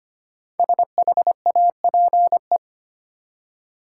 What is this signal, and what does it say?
Morse code, heard as a keyed tone, about 25 wpm, 700 Hz. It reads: SHAPE
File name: 5 Letter Words - First 3 Letters Given - 25wpm